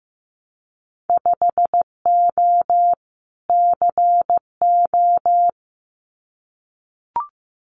Morse code, heard as a keyed tone, about 15 wpm, 700 Hz. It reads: 5O CO
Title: Winter Field Day Exchanges Encore 15wpm